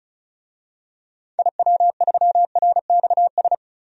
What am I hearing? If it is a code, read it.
Morse code, heard as a keyed tone, about 34 wpm, 700 Hz. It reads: IW3RXS
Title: DXCC-100 Call Signs - 6 Characters - Repeated 3x - 35wpm